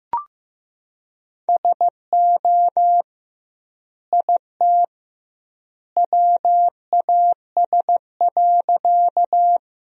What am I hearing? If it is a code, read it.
Morse code, heard as a keyed tone, about 15 wpm, 700 Hz. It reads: SO IT WAS.
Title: 2x Word-Spacing - Sentences from Top 100 Words - 15wpm